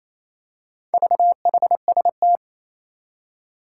Morse code, heard as a keyed tone, about 27 wpm, 700 Hz. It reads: VHST
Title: Four Challenging Letter-Number Combinations 28wpm